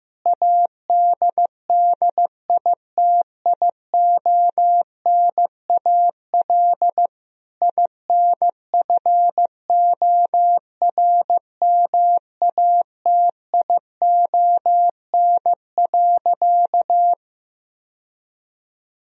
Morse code, heard as a keyed tone, about 15 wpm, 700 Hz. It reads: ADDITIONAL INFORMATION.
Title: Sentences from Top 1000 Words - Review Part 4 - 15wpm